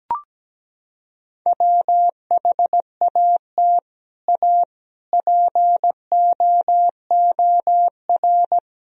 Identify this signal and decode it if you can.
Morse code, heard as a keyed tone, about 17 wpm, 700 Hz. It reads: WHAT A POOR
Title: Sentences from Top 1000 Words - Review Part 1 - 17wpm